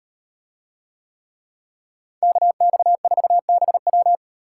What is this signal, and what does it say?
Morse code, heard as a keyed tone, about 38 wpm, 700 Hz. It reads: KX4BW